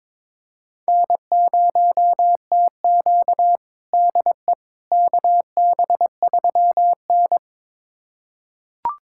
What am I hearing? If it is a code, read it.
Morse code, heard as a keyed tone, about 22 wpm, 700 Hz. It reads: N0TQ DE KB3N
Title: USA Callsign DE Callsign - 4 Characters - 22wpm